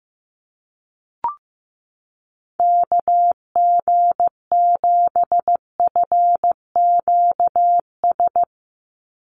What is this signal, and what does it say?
Morse code, heard as a keyed tone, about 15 wpm, 700 Hz. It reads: KG7FQS
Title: USA Call Signs 15wpm